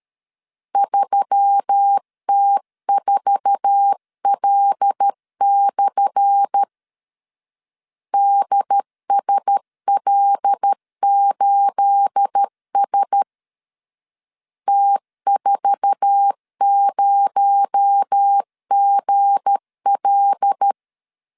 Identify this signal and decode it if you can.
Morse code, heard as a keyed tone, about 13 wpm, 800 Hz. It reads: 3T4L/ DSL8S T40GL
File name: LEZIONI / 10 SIGNS / 10B INT 12wpm 2414